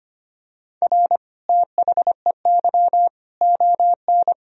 Morse code, heard as a keyed tone, about 25 wpm, 700 Hz. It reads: R THEY ON
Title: Sentences from Top 500 Words and Abbreviations 25wpm